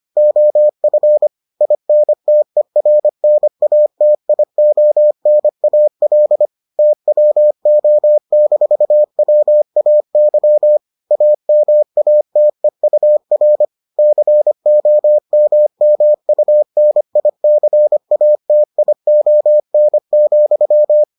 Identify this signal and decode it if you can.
Morse code, heard as a keyed tone, about 25 wpm, 600 Hz. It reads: OF INTERNATIONAL TWO-WAY AMATEUR COMMUNICATION,